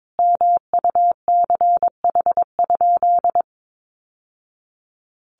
Morse code, heard as a keyed tone, about 22 wpm, 700 Hz. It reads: MUCH?
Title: Sentences from Top 1200 Words 22wpm